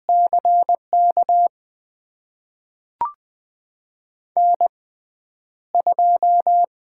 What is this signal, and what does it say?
Morse code, heard as a keyed tone, about 20 wpm, spaced as for 10 wpm, 700 Hz. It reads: CK N2
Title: USA Call Signs - 5 Characters - 6x Character Spacing - 20wpm